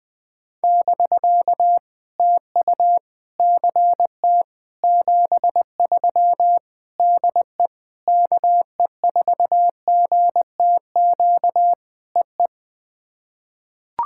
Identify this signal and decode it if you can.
Morse code, heard as a keyed tone, about 20 wpm, 700 Hz. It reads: <BK> TU CT 73 DE KE4GTQ EE
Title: POTA QSO Practice - 20wpm